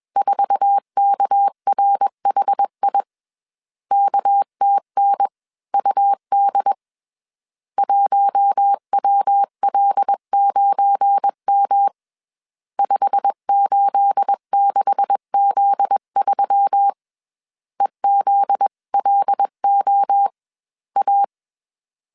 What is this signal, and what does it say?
Morse code, heard as a keyed tone, about 21 wpm, spaced as for 19 wpm, 800 Hz. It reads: 4KRHI KTNUD 1WL9M 586Z3 EZLOA